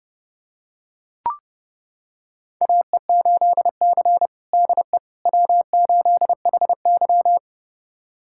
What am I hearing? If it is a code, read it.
Morse code, heard as a keyed tone, about 30 wpm, 700 Hz. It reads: AE8C DE W8HY
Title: USA Callsign DE Callsign - 4 Characters - 30wpm